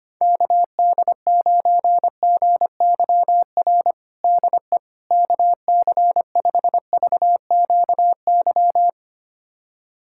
Morse code, heard as a keyed tone, about 25 wpm, 700 Hz. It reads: KD9GYR DE KC5VQY
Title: USA Callsign DE Callsign - 6 Characters - 25wpm